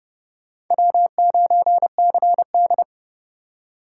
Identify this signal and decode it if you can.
Morse code, heard as a keyed tone, about 26 wpm, 700 Hz. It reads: W9CD